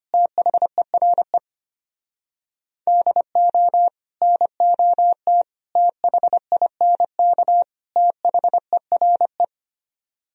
Morse code, heard as a keyed tone, about 32 wpm, 700 Hz. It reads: THERE DO NOT THINK THERE